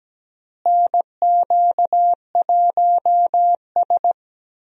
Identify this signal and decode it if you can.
Morse code, heard as a keyed tone, about 17 wpm, 700 Hz. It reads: NQ1S
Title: USA Call Signs - 4 Characters - Repeated 2x - 17wpm